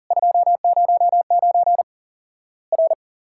Morse code, heard as a keyed tone, about 41 wpm, 700 Hz. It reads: 209 R
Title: Worked All Europe DX Contest - QTC Exchange - 40wpm